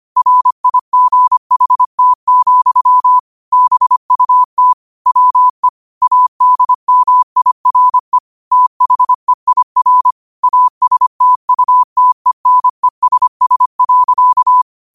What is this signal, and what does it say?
Morse code, heard as a keyed tone, about 25 wpm, 1000 Hz. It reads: RIGHT, BUT WE ADMIRE THEIR ASTUTENESS.
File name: qotdfiles / 25wpm